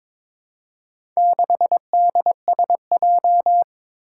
Morse code, heard as a keyed tone, about 22 wpm, 700 Hz. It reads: 6DSJ
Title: Four Challenging Letter-Number Combinations 22wpm